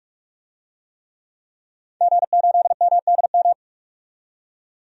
Morse code, heard as a keyed tone, about 46 wpm, 700 Hz. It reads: G8MDM
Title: DXCC-100 Call Signs - 5 Characters - 45wpm